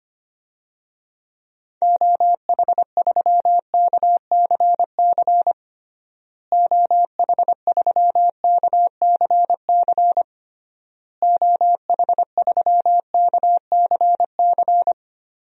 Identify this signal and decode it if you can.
Morse code, heard as a keyed tone, about 25 wpm, 700 Hz. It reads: OH3KCC OH3KCC OH3KCC
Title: DXCC-100 Call Signs - 6 Characters - Repeated 3x - 25wpm